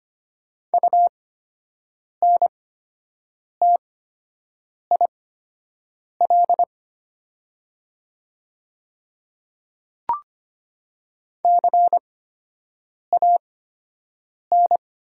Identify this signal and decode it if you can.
Morse code, heard as a keyed tone, about 25 wpm, spaced as for 7 wpm, 700 Hz. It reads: UNTIL CAN
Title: 8x Character Spacing - Top 500 Words - 25wpm